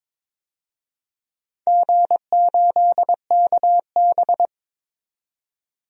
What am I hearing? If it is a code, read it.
Morse code, heard as a keyed tone, about 21 wpm, 700 Hz. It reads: G8KB
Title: DXCC-100 Call Signs - 4 Characters - 22wpm